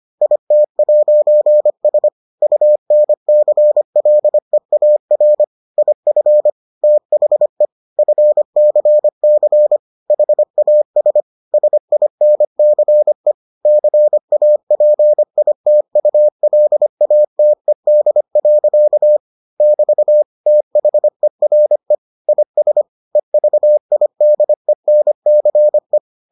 Morse code, heard as a keyed tone, about 25 wpm, 600 Hz. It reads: IT'S UNCLEAR IF THE FCC HAS SINCE CAPITULATED. <BT> THERE IS EVIDENCE